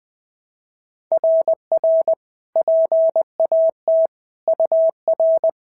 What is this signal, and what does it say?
Morse code, heard as a keyed tone, about 18 wpm, 650 Hz. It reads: RR PAT UR
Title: SOTA QSO Practice 20wpm